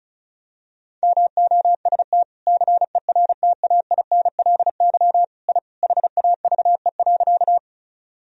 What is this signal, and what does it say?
Morse code, heard as a keyed tone, about 36 wpm, 700 Hz. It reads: MOST CERTAINLY I HAVE.
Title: Sentences from Top 1000 Words - Review Part 1 - 35wpm